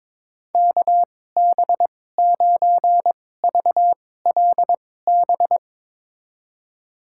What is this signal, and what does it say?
Morse code, heard as a keyed tone, about 22 wpm, spaced as for 16 wpm, 700 Hz. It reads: KB9VLB
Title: USA Call Signs - 6 Characters - 2x Character Spacing - 22wpm